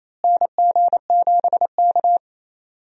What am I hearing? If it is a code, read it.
Morse code, heard as a keyed tone, about 28 wpm, 700 Hz. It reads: NG7K
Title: USA Call Signs - 4 Characters - Repeated 2x - 28wpm